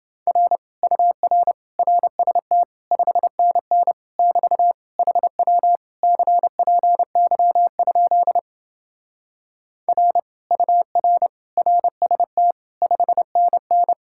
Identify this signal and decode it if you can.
Morse code, heard as a keyed tone, about 27 wpm, 700 Hz. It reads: R UR RST 5NN <BT> HW CPY? R UR RST 5NN